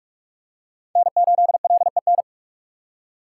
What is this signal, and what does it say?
Morse code, heard as a keyed tone, about 41 wpm, 700 Hz. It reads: N8LEN